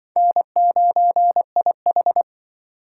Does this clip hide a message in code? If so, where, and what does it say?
Yes: Morse code from 0.2 to 2.2 s, heard as a keyed tone, about 26 wpm, 700 Hz. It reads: N9IH